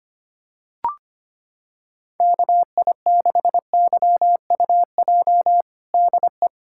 Morse code, heard as a keyed tone, about 25 wpm, 700 Hz. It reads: KI6YUJ DE
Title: USA Callsign DE Callsign 25wpm